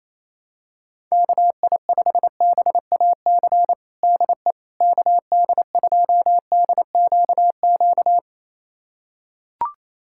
Morse code, heard as a keyed tone, about 28 wpm, 700 Hz. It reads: KI5BAC DE KD2DQQ
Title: USA Callsign DE Callsign - 6 Characters - 28wpm